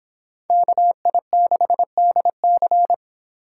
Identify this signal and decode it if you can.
Morse code, heard as a keyed tone, about 25 wpm, 700 Hz. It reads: KI6DC